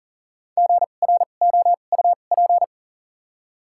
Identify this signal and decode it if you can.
Morse code, heard as a keyed tone, about 40 wpm, 700 Hz. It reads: G R O U P
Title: 2x Character Spacing - Top 300 Words - 40wpm